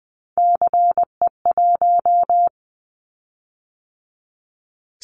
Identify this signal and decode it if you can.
Morse code, heard as a keyed tone, about 20 wpm, 700 Hz. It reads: CE1
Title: Callsign Prefixes for Top 100 DXCC - 20wpm